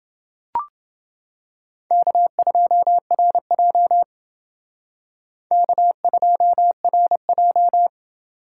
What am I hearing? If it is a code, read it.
Morse code, heard as a keyed tone, about 29 wpm, 700 Hz. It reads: K2RJ K2RJ